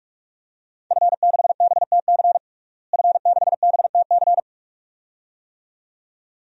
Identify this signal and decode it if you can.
Morse code, heard as a keyed tone, about 46 wpm, 700 Hz. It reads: F6BTC F6BTC
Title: DXCC-100 Call Signs - 5 Characters - Repeated 2x - 45wpm